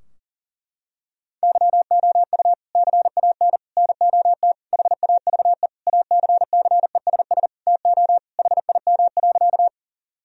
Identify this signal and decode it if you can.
Morse code, heard as a keyed tone, about 37 wpm, 700 Hz. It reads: YOU CAN NOT HAVE ACCESS TO HIM.
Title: Sentences from Top 1000 Words - Review Part 4 - 40wpm